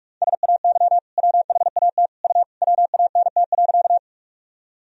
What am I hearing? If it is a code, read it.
Morse code, heard as a keyed tone, about 41 wpm, 700 Hz. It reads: SAY WHAT U WANT.
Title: Sentences from Top 200 Words and Abbreviations 45wpm